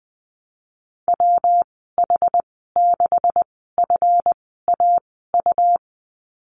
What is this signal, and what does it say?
Morse code, heard as a keyed tone, about 20 wpm, spaced as for 14 wpm, 700 Hz. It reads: WH6FAU
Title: 147 Lesson 24g - Letter y - Call Signs 20wpm